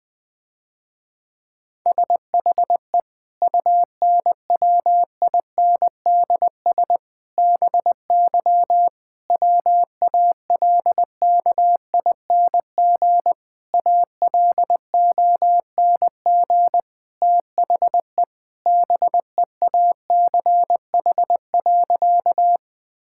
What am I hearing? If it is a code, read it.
Morse code, heard as a keyed tone, about 20 wpm, 700 Hz. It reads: SHE UNWINDS BY WALKING ALONG THE BEACH.
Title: Difficult Sentences with Hints - 20wpm